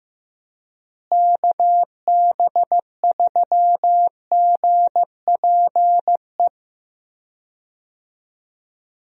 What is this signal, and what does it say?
Morse code, heard as a keyed tone, about 15 wpm, 700 Hz. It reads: KB3GPE
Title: USA Call Signs - 6 Characters - Repeated 2x - 15wpm